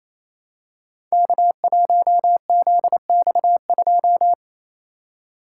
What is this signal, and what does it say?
Morse code, heard as a keyed tone, about 29 wpm, 700 Hz. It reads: K1ZX2
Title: Code Groups - 5 Letter:Numbers - 28wpm